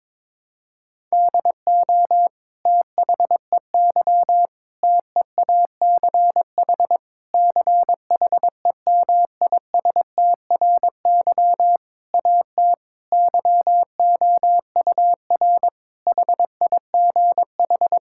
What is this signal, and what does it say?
Morse code, heard as a keyed tone, about 22 wpm, 700 Hz. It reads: DO THEY TEACH CHEMISTRY AT YOUR HIGH
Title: Sentences from Top 1000 Words - Review Part 6 - 22wpm